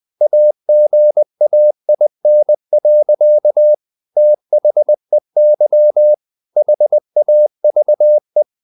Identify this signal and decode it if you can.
Morse code, heard as a keyed tone, about 20 wpm, 600 Hz. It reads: AGAIN. THEY HAVE
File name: the old man and the sea / the old man and the sea part 1 20wpm